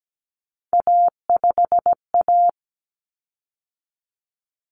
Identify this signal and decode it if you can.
Morse code, heard as a keyed tone, about 17 wpm, 700 Hz. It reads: A5A
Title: Common Cut Numbers - 3-Digit 17wpm